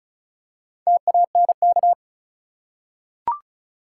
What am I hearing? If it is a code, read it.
Morse code, heard as a keyed tone, about 36 wpm, 700 Hz. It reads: TANK
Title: Four Letter Words - 35wpm